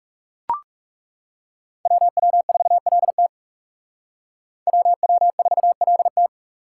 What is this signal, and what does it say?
Morse code, heard as a keyed tone, about 37 wpm, 700 Hz. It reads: WW4LT WW4LT
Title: Speed Racing - USA 2x2 Call Signs 30wpm